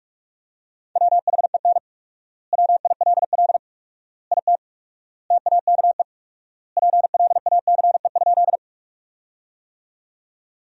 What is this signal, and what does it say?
Morse code, heard as a keyed tone, about 46 wpm, 700 Hz. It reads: WHEN WILL IT TAKE PLACE?
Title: 4x Word-Spacing - Sentences from Top 200 Words - 45wpm